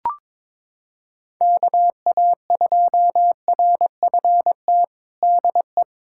Morse code, heard as a keyed tone, about 22 wpm, 700 Hz. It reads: KA2RFT DE
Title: USA Callsign DE Callsign 22wpm